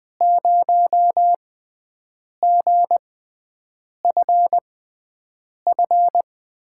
Morse code, heard as a keyed tone, about 20 wpm, spaced as for 7 wpm, 700 Hz. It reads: 0GFF
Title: USA Call Signs - 6 Characters - 6x Character Spacing - 20wpm